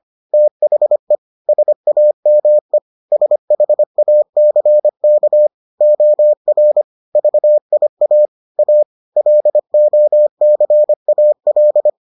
Morse code, heard as a keyed tone, about 25 wpm, 600 Hz. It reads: THE SAME SHACK OR VIA A LOCAL